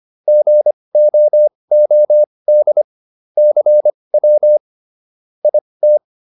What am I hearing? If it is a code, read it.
Morse code, heard as a keyed tone, about 25 wpm, spaced as for 20 wpm, 600 Hz. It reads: GOOD CW IT